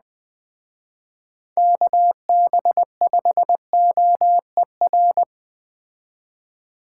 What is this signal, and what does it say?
Morse code, heard as a keyed tone, about 20 wpm, 700 Hz. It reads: KB5OER